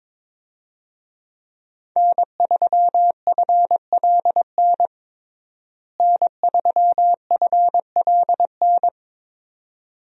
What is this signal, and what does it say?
Morse code, heard as a keyed tone, about 22 wpm, 700 Hz. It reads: N3FLN N3FLN
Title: USA Call Signs - 5 Characters - Repeated 2x - 22wpm